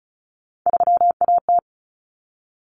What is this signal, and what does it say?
Morse code, heard as a keyed tone, about 34 wpm, 700 Hz. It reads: 3AT
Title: Common Cut Numbers - 3-Digit 35wpm